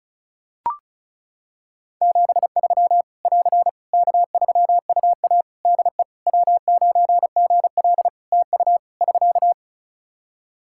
Morse code, heard as a keyed tone, about 34 wpm, 700 Hz. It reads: 73 <AR> K3UA DE W9GL TU <SK>
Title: Practice QSO - Exchange 4 - Ending - 35wpm